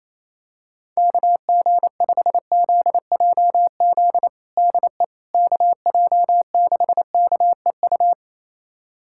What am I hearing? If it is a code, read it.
Morse code, heard as a keyed tone, about 28 wpm, 700 Hz. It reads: KG5ZJZ DE KJ6KEU
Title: USA Callsign DE Callsign - 6 Characters - 28wpm